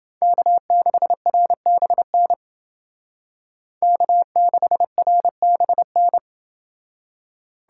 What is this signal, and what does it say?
Morse code, heard as a keyed tone, about 29 wpm, 700 Hz. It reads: K6RBN K6RBN